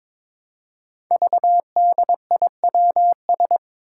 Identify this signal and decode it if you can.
Morse code, heard as a keyed tone, about 22 wpm, 700 Hz. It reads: VDIWS
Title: Five Challenging Letter-Number Combinations 22wpm